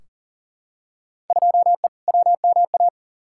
Morse code, heard as a keyed tone, about 38 wpm, 700 Hz. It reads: 2E WMA
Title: Field Day Exchanges Encore - 40wpm